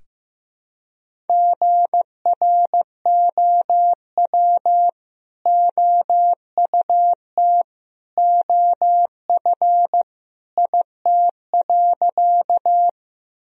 Morse code, heard as a keyed tone, about 15 wpm, 700 Hz. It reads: GROW OUT OF IT.
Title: Sentences from Top 1000 Words - Review Part 1 - 15wpm